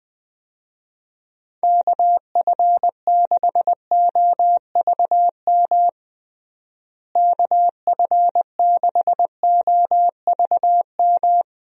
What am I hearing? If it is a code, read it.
Morse code, heard as a keyed tone, about 20 wpm, 700 Hz. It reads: KF6OVM KF6OVM
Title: USA Call Signs - 6 Characters - Repeated 2x - 20wpm